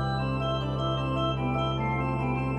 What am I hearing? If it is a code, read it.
There is no code.